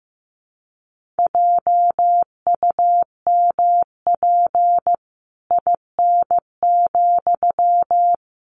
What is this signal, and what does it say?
Morse code, heard as a keyed tone, about 15 wpm, 700 Hz. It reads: JUMP IN,
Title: Sentences from Top 1300 Words 15wpm